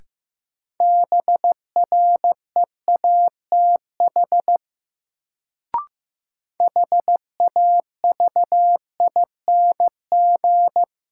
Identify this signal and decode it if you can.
Morse code, heard as a keyed tone, about 16 wpm, 700 Hz. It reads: BREATH HAVING
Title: Sets of 5 Words Encore - New Word Drill - 15wpm